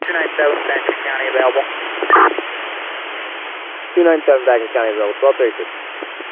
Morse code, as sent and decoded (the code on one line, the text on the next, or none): none